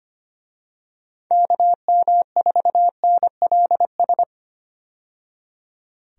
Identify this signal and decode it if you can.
Morse code, heard as a keyed tone, about 25 wpm, 700 Hz. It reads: KM4NLS
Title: USA Call Signs - 6 Characters - 25wpm